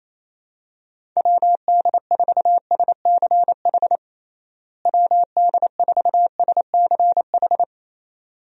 Morse code, heard as a keyed tone, about 28 wpm, 700 Hz. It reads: WD4SCH WD4SCH
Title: USA Call Signs - 6 Characters - Repeated 2x - 28wpm